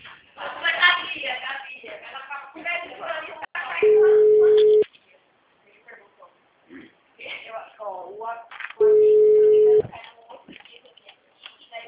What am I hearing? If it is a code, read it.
Ringback tone (Europe, 425 Hz), with speech.
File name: out- -2042- - - .81758